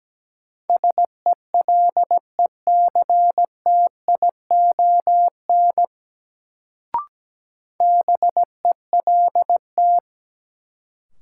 Morse code, heard as a keyed tone, about 17 wpm, 700 Hz. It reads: SELECTION BELT
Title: Sentences from Top 1400 Words - New Word Drill Alt - 17wpm